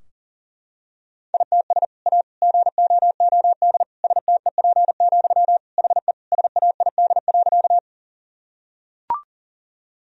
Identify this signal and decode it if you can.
Morse code, heard as a keyed tone, about 40 wpm, 700 Hz. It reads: ITS A GOOD STEP, HE SAID.
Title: Sentences from Top 1000 Words - Review Part 2 - 40wpm